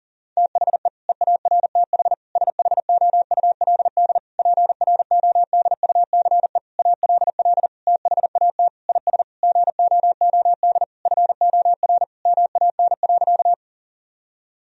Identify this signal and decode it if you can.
Morse code, heard as a keyed tone, about 40 wpm, 700 Hz. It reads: THE EARTH SHOULD PRODUCE ALL THAT IS GOOD FOR MAN.
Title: Sentences from Top 1000 Words - Review Part 3 - 40wpm